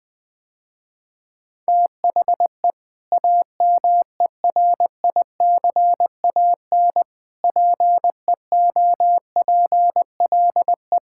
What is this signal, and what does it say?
Morse code, heard as a keyed tone, about 20 wpm, 700 Hz. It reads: THE AMERICAN PEOPLE
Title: Sentences with 3-Word Phrases - Spoken Beginning - 20wpm